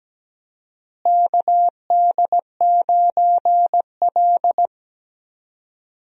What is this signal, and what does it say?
Morse code, heard as a keyed tone, about 17 wpm, 700 Hz. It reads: KD9L
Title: USA Call Signs - 4 Characters - Repeated 3x - 17wpm